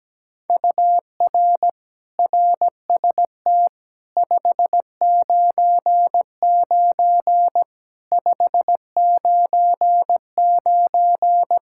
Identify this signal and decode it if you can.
Morse code, heard as a keyed tone, about 17 wpm, 700 Hz. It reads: UR RST 599 599
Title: Practice QSO - Exchange 1 - Beginning - 17wpm